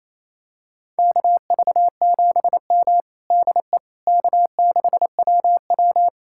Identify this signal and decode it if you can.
Morse code, heard as a keyed tone, about 28 wpm, 700 Hz. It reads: KV7M DE K6WW
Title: USA Callsign DE Callsign - 4 Characters - 28wpm